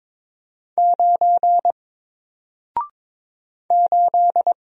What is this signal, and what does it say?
Morse code, heard as a keyed tone, about 22 wpm, 700 Hz. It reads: 9 8
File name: Single-Letter-Number-22wpm